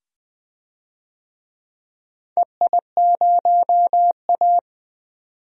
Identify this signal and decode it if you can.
Morse code, heard as a keyed tone, about 20 wpm, 700 Hz. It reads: EI0A
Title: DXCC-100 Call Signs - 4 Characters - 20wpm